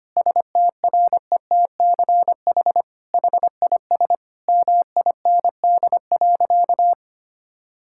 Morse code, heard as a keyed tone, about 25 wpm, 700 Hz. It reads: STRETCH HIS MIND.
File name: practice / 25wpm / All-Age Appropriate Jokes - 25wpm